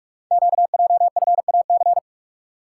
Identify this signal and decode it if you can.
Morse code, heard as a keyed tone, about 44 wpm, 690 Hz. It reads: QJFAC
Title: Code Groups - 5 Letters - 45wpm